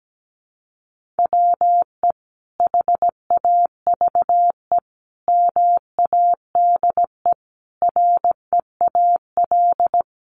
Morse code, heard as a keyed tone, about 17 wpm, 700 Hz. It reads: WE HAVE MADE REAL
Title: Sentences from Top 1400 Words 17wpm